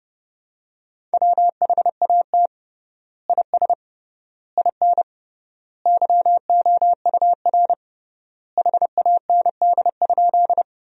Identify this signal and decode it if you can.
Morse code, heard as a keyed tone, about 30 wpm, 700 Hz. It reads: WHAT IS IN YOUR HAND?
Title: 3x Word-Spacing - Sentences from Top 200 Words - 30wpm